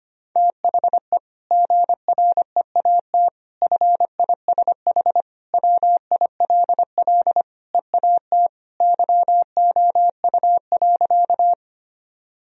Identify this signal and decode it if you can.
Morse code, heard as a keyed tone, about 25 wpm, 700 Hz. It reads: THE GREAT FISH WILL EAT YOU.
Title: Sentences from Top 1000 Words - Review Part 1 - 25wpm